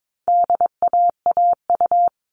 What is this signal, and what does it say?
Morse code, heard as a keyed tone, about 22 wpm, 700 Hz. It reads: DAAU
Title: All Cut Numbers - 4-Digit 22wpm